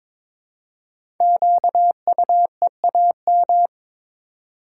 Morse code, heard as a keyed tone, about 22 wpm, 700 Hz. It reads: QUEAM
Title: Code Groups - 5 Letters - 22wpm